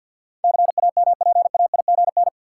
Morse code, heard as a keyed tone, about 46 wpm, 700 Hz. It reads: CAMPAIGN
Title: Sentences from Top 1200 Words - New Word Drill - 50wpm